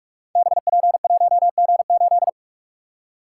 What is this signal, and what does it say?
Morse code, heard as a keyed tone, about 44 wpm, 700 Hz. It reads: BP1G8
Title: Code Groups - 5 Letter:Numbers - 45wpm